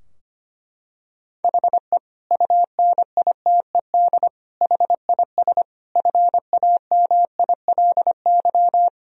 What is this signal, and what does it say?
Morse code, heard as a keyed tone, about 25 wpm, 700 Hz. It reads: HE UNITED HIS FAMILY
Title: Sentences from Top 1000 Words - Review Part 4 - 25wpm